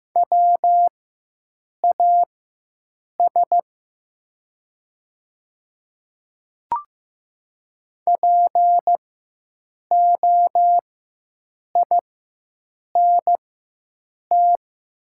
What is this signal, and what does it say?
Morse code, heard as a keyed tone, about 15 wpm, spaced as for 7 wpm, 700 Hz. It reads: WAS POINT
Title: 4x Character Spacing - Top 200 Words - 15wpm